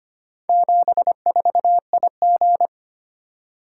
Morse code, heard as a keyed tone, about 25 wpm, 700 Hz. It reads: 74IG
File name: practice / 25wpm / Four Letter-Number Combinations - 25wpm